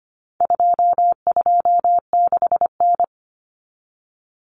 Morse code, heard as a keyed tone, about 25 wpm, 700 Hz. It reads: 226N
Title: Common Cut Numbers - 4-Digit 25wpm